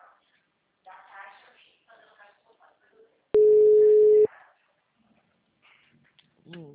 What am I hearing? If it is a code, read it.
Ringback tone (Europe, 425 Hz), with speech.